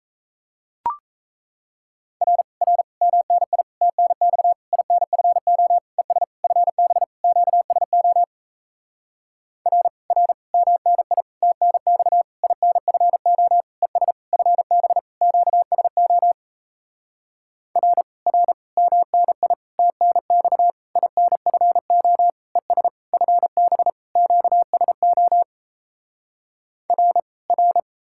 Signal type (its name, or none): Morse code